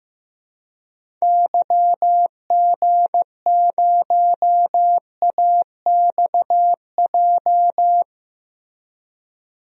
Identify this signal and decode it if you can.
Morse code, heard as a keyed tone, about 15 wpm, 700 Hz. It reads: YG0AXJ
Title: DXCC-100 Call Signs - 6 Characters - Repeated 3x - 15wpm